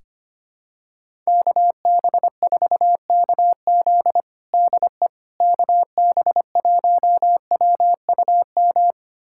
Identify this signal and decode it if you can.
Morse code, heard as a keyed tone, about 25 wpm, 700 Hz. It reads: KB4KZ DE KB1WUM
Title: USA Callsign DE Callsign 25wpm